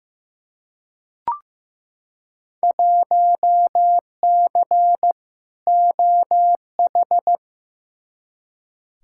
Morse code, heard as a keyed tone, about 15 wpm, 700 Hz. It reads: 1C OH
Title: Field Day Exchanges - 15wpm